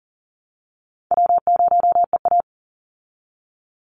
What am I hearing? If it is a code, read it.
Morse code, heard as a keyed tone, about 40 wpm, 700 Hz. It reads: W0EA
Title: Easy Call Signs Repeated 2x - 40wpm